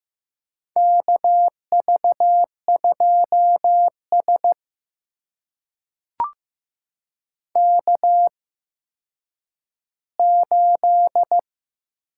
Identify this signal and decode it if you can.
Morse code, heard as a keyed tone, about 15 wpm, spaced as for 9 wpm, 700 Hz. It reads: KV2S K8
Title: USA Call Signs - 4 Characters - 8x Character Spacing - 15wpm